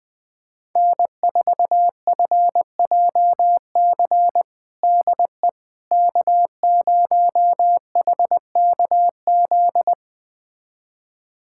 Morse code, heard as a keyed tone, about 21 wpm, 700 Hz. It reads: N4FJC DE K0HKZ
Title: USA Callsign DE Callsign - 5 Characters - 20wpm